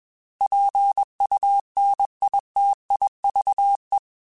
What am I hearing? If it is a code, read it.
Morse code, heard as a keyed tone, about 21 wpm, 800 Hz. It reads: PUNITIVE
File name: Alphabetical / Alphabetical mp3s / Ps @20